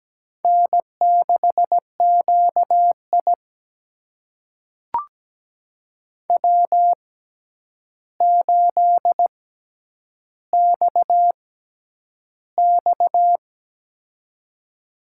Morse code, heard as a keyed tone, about 17 wpm, spaced as for 9 wpm, 700 Hz. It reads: N6QI W8XX
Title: USA Call Signs - 4 Characters - 6x Character Spacing - 17wpm